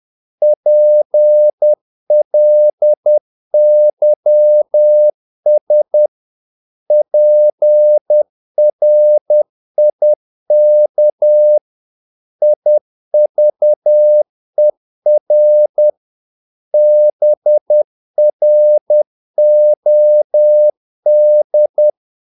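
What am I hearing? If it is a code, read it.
Morse code, heard as a keyed tone, about 10 wpm, 600 Hz. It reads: PLYS PRIK IVER BROD
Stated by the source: Morse practice of short Danish words at 10 wpm, from CW med Gnister.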